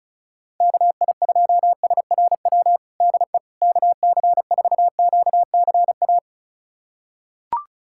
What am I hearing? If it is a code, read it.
Morse code, heard as a keyed tone, about 36 wpm, 700 Hz. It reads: KI2SRW DE KC4QCA
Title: USA Callsign DE Callsign 35wpm